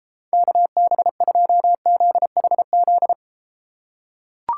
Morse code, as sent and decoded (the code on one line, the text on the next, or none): -.- -... ..--- --.. .... --..
KB2ZHZ